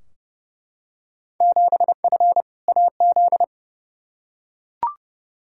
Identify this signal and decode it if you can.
Morse code, heard as a keyed tone, about 30 wpm, 700 Hz. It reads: 7F AZ
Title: Field Day Exchanges Encore - 30wpm